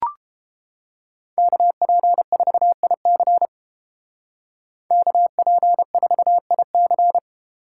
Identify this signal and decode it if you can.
Morse code, heard as a keyed tone, about 29 wpm, 700 Hz. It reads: KP4IC KP4IC